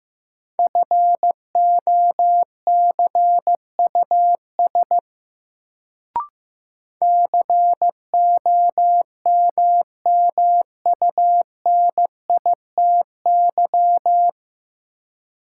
Morse code, heard as a keyed tone, about 15 wpm, 700 Hz. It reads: FOCUS COMMUNITY
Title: Sentences from Top 1000 Words - Review Part 4 - New Word Drill - 15wpm